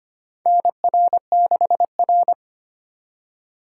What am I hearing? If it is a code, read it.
Morse code, heard as a keyed tone, about 25 wpm, 700 Hz. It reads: NR6R
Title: USA Call Signs - 4x Character Spacing - 25wpm